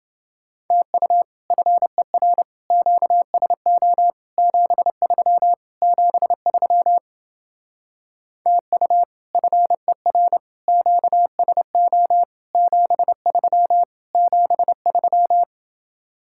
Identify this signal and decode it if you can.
Morse code, heard as a keyed tone, about 28 wpm, 700 Hz. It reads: TU FER QSO 73 73 TU FER QSO 73 73